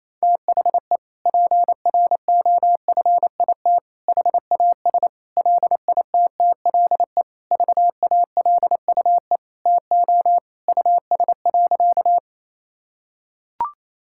Morse code, heard as a keyed tone, about 27 wpm, 700 Hz. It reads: THE PROFIT HAS LITTLE VALUE TO US.
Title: Sentences from Top 1000 Words - Review Part 5 - 28wpm